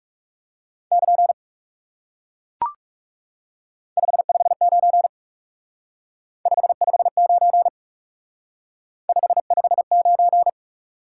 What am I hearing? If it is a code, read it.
Morse code, heard as a keyed tone, about 37 wpm, 700 Hz. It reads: <KN> 559 559 559
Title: Speed-Racing Top 90 QSO Elements 30wpm